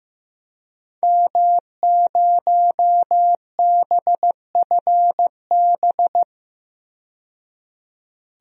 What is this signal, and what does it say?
Morse code, heard as a keyed tone, about 15 wpm, 700 Hz. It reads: M0BFB
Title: DXCC-100 Call Signs - 5 Characters - 15wpm